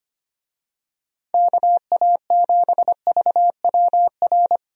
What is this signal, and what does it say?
Morse code, heard as a keyed tone, about 25 wpm, 700 Hz. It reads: KA7VWR
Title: USA Call Signs - 6 Characters - 25wpm